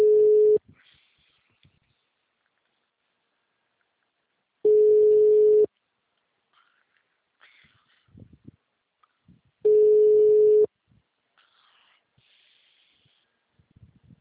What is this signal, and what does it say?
Ringback tone (Europe, 425 Hz).
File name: out- -2007- - - .675